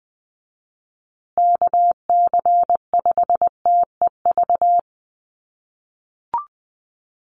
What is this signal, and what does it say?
Morse code, heard as a keyed tone, about 20 wpm, 700 Hz. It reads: KC5TEV
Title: USA Call Signs Repeated 2x 20wpm